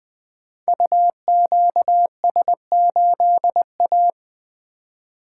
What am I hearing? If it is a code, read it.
Morse code, heard as a keyed tone, about 20 wpm, 700 Hz. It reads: UQS8A
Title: Code Groups - 5 Letter:Numbers - 20wpm